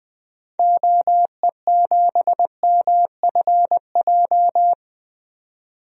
Morse code, heard as a keyed tone, about 20 wpm, 700 Hz. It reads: OE7MFJ
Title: DXCC-100 Call Signs - 6 Characters - 20wpm